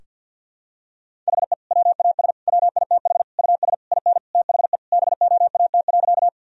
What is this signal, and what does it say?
Morse code, heard as a keyed tone, about 50 wpm, 700 Hz. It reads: HE WAS WITH US IN THE BOAT.
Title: Sentences from Top 1000 Words - Review Part 3 - 50wpm